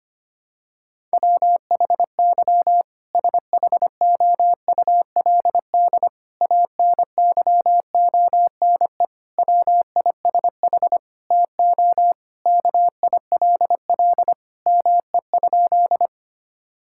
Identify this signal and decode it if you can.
Morse code, heard as a keyed tone, about 25 wpm, 700 Hz. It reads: WHY SHOULD ANYONE WISH TO KILL ME?
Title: Sentences from Top 1600 Words 25wpm